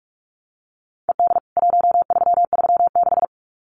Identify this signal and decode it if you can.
Morse code, heard as a keyed tone, about 44 wpm, 700 Hz. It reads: ED 1336
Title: CWT Exchange 45wpm